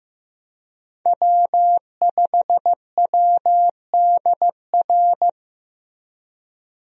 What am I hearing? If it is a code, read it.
Morse code, heard as a keyed tone, about 16 wpm, 700 Hz. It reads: W5WDR
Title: USA Call Signs - 5 Characters - Repeated 2x - 15wpm